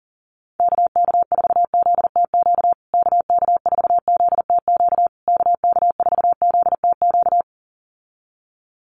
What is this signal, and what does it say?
Morse code, heard as a keyed tone, about 40 wpm, 700 Hz. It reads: KK4ZTQ KK4ZTQ KK4ZTQ
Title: USA Call Signs Repeated 3x 40wpm